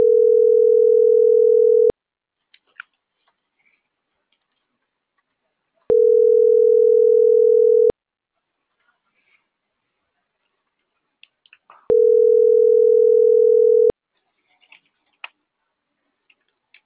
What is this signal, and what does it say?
Ringback tone (North America), 440 + 480 Hz.